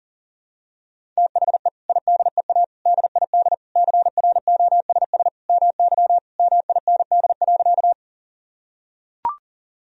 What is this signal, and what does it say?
Morse code, heard as a keyed tone, about 40 wpm, 700 Hz. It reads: THE IDEA DID CROSS MY MIND.
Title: Sentences from Top 1000 Words - Review Part 1 - 40wpm